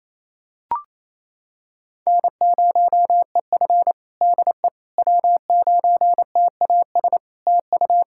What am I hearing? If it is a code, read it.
Morse code, heard as a keyed tone, about 28 wpm, 700 Hz. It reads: N0EF DE W9TAS TU
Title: Practice QSO - Exchange 4 - Ending - 28wpm